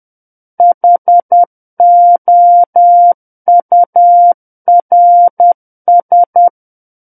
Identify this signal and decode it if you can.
Morse code, heard as a keyed tone, about 10 wpm, 700 Hz. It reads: HOURS
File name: Book One / 01-Chapter 1 - 10 WPM